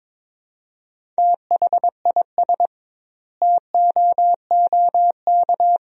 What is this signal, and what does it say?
Morse code, heard as a keyed tone, about 22 wpm, 700 Hz. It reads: THIS TOOK
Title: 2x Word-Spacing - Sentences from Top 300 Words - 22wpm